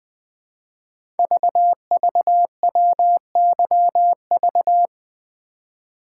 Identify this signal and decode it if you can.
Morse code, heard as a keyed tone, about 20 wpm, 700 Hz. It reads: VVWYV
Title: Code Groups - 5 Letters - 20wpm